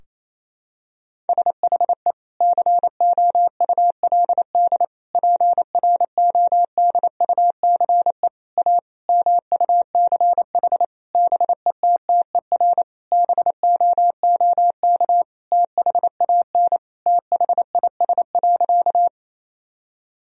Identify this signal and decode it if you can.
Morse code, heard as a keyed tone, about 28 wpm, 700 Hz. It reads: SHE COULD PRODUCE A MUCH BETTER BOOK THAN THIS.
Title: Sentences from Top 1000 Words - Review Part 3 - 28wpm